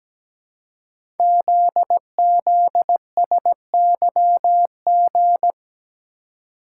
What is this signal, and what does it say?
Morse code, heard as a keyed tone, about 17 wpm, 700 Hz. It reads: ZZSYG
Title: Code Groups - 5 Letters - 17wpm